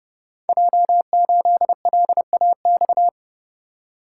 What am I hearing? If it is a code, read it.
Morse code, heard as a keyed tone, about 30 wpm, 700 Hz. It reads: J8LAX
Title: Code Groups - 5 Letter:Numbers - 30wpm